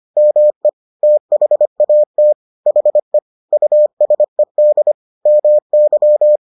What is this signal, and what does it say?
Morse code, heard as a keyed tone, about 25 wpm, 600 Hz. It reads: ME THAT HE USED MY